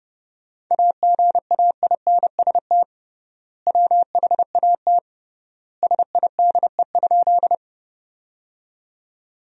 Morse code, heard as a keyed tone, about 30 wpm, 700 Hz. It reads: AGAINST WHAT SIDE?
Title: 3x Word-Spacing - Sentences from Top 400 Words - 30wpm